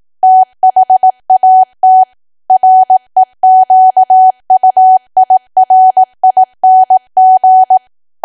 Morse code, heard as a keyed tone, about 18 wpm, 750 Hz. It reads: THAT REQUIRING
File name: cwa / W1AW-Level2 / 18WPM